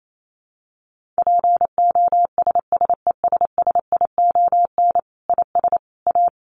Morse code, heard as a keyed tone, about 28 wpm, 700 Hz. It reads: POSSESSION IS A
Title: Sentences from Top 1400 Words 28wpm